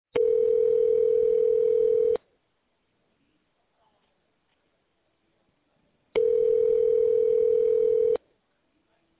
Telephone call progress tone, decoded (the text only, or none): ringback tone